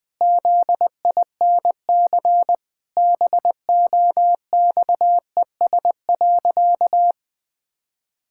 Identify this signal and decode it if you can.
Morse code, heard as a keyed tone, about 20 wpm, 700 Hz. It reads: ZINC BOXES.
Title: Pangram Challenge v2 - 20wpm